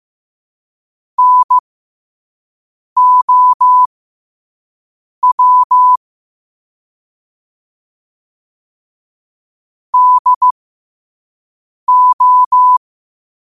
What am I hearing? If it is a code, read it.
Morse code, heard as a keyed tone, about 15 wpm, spaced as for 5 wpm, 1000 Hz. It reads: NOW DO
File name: qotdfiles / 5wpm